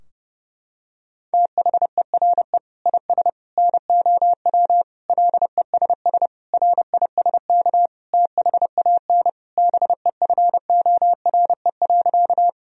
Morse code, heard as a keyed tone, about 30 wpm, 700 Hz. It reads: THERE IS NOW LESS RISK THAN BEFORE.
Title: Sentences from Top 1000 Words - Review Part 4 - 30wpm